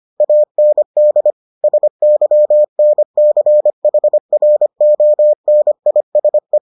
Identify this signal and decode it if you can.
Morse code, heard as a keyed tone, about 25 wpm, 600 Hz. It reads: AND SYNCHRONISE